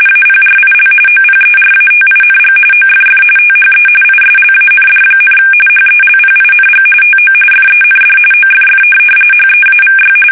RTTY, about 45.45 baud. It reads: APRIL OPERATING OFF THE ISLAND, HELPING TO FEND OFF HEAVY
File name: fsk850 / USS-Massachusetts